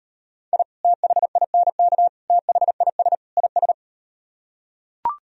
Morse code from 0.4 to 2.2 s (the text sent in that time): I THINK